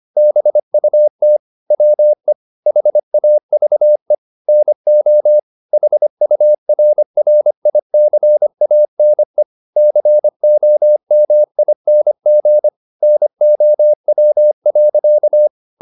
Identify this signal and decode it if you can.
Morse code, heard as a keyed tone, about 25 wpm, 600 Hz. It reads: BUT WE HAVE NO HURRICANE COMING NOW.
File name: the old man and the sea part 1 25wpm